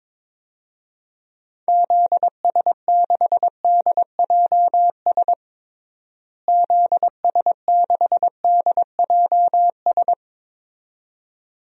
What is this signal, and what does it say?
Morse code, heard as a keyed tone, about 22 wpm, 700 Hz. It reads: ZS6DJS ZS6DJS
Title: DXCC-100 Call Signs - 6 Characters - Repeated 2x - 22wpm